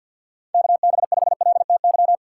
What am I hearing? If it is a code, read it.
Morse code, heard as a keyed tone, about 52 wpm, 700 Hz. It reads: KB5LTY
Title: USA Call Signs - 6 Characters - 6x Character Spacing - 50wpm